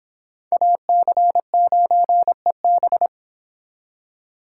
Morse code, heard as a keyed tone, about 30 wpm, 700 Hz. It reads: AC9EB